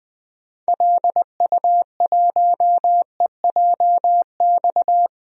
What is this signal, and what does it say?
Morse code, heard as a keyed tone, about 20 wpm, 700 Hz. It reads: LU1EJX
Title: DXCC-100 Call Signs - 6 Characters - 20wpm